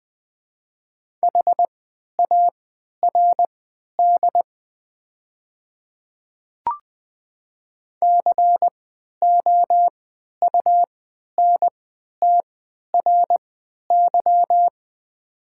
Morse code, heard as a keyed tone, about 21 wpm, spaced as for 11 wpm, 700 Hz. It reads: HARD COUNTRY
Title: 3x Character Spacing - Top 300 Words - 20wpm